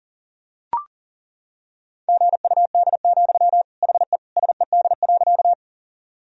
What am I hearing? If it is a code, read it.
Morse code, heard as a keyed tone, about 40 wpm, 700 Hz. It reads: GUD, HE SED.
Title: Sentences from Top 200 Words and Abbreviations 40wpm